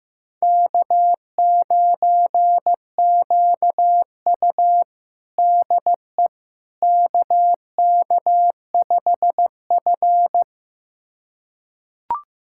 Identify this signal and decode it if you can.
Morse code, heard as a keyed tone, about 15 wpm, 700 Hz. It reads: K9QU DE KK5F
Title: USA Callsign DE Callsign - 4 Characters - 15wpm